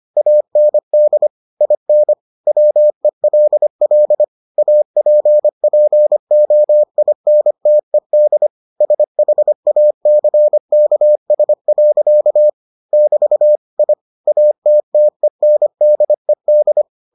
Morse code, heard as a keyed tone, about 25 wpm, 600 Hz. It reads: AND IN WELL APPOINTED SHACKS. <BT> I ATTENDED